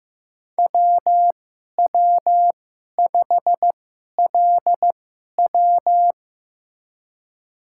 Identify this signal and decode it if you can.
Morse code, heard as a keyed tone, about 15 wpm, spaced as for 11 wpm, 700 Hz. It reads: WW5LW
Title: USA Call Signs - 2x Character Spacing - 15wpm